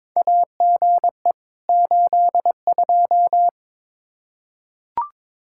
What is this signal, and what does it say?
Morse code, heard as a keyed tone, about 22 wpm, 700 Hz. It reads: AGE 82
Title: K1USN SST QSO Practice 22wpm